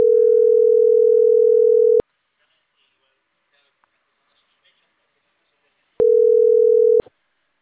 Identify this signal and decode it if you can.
Ringback tone (North America), 440 + 480 Hz, with speech.
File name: CENTRAL CAEA / out agcaea10